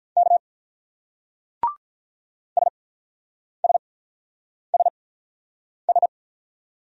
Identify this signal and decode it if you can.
Morse code, heard as a keyed tone, about 44 wpm, spaced as for 9 wpm, 700 Hz. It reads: K SSSS